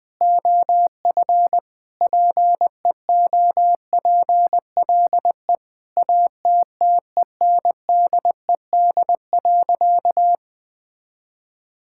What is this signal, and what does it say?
Morse code, heard as a keyed tone, about 20 wpm, 700 Hz. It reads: OF PEOPLE ATTENDED.
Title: Sentences with 4-Word Phrases - 20wpm